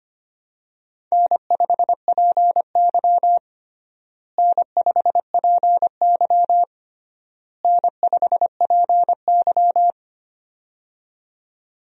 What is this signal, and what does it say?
Morse code, heard as a keyed tone, about 25 wpm, 700 Hz. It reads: N5PY N5PY N5PY
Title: USA Call Signs - 4 Characters - Repeated 3x - 25wpm